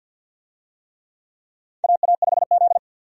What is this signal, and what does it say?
Morse code, heard as a keyed tone, about 53 wpm, 700 Hz. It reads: AA5Z
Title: USA Call Signs - 4 Characters - 50wpm